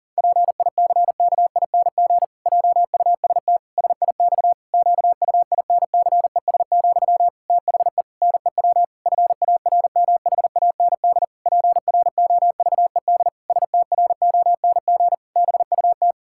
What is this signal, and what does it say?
Morse code, heard as a keyed tone, about 40 wpm, 700 Hz. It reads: PICKING JUST SIX QUINCES, THE NEW FARMHAND PROVED STRONG BUT
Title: Pangram Challenge v2 - 40wpm